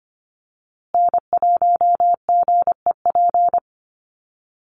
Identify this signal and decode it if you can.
Morse code, heard as a keyed tone, about 25 wpm, 700 Hz. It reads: N1GEP
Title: USA Call Signs Repeated 3x 25wpm